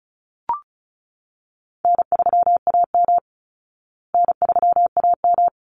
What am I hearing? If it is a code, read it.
Morse code, heard as a keyed tone, about 34 wpm, 700 Hz. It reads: N3AM N3AM
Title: Easy Call Signs Repeated 3x - 35wpm